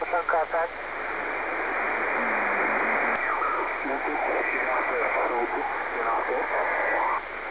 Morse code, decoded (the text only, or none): none